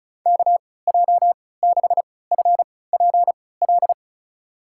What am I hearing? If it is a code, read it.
Morse code, heard as a keyed tone, about 36 wpm, spaced as for 20 wpm, 700 Hz. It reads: KJ6FPL
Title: USA Call Signs - 6 Characters - 3x Character Spacing - 35wpm